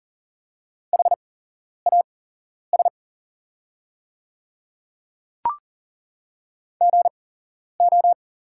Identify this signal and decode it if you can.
Morse code, heard as a keyed tone, about 40 wpm, spaced as for 11 wpm, 700 Hz. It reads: HAS GO
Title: 8x Character Spacing - Top 100 Words - 40wpm